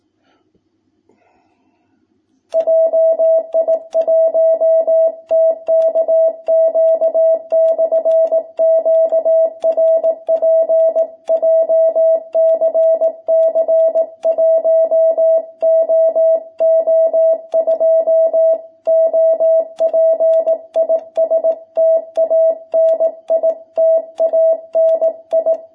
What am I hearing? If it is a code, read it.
Morse code, heard as a keyed tone, about 18 wpm, 660 Hz. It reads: JI1TKQ/QRP JCC 1OO2 OP IS TANI TANI
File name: jh8cbh / 2021cw